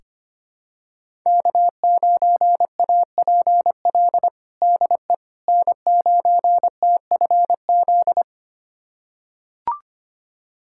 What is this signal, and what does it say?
Morse code, heard as a keyed tone, about 25 wpm, 700 Hz. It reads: K9APL DE N9TFZ
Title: USA Callsign DE Callsign - 5 Characters - 25wpm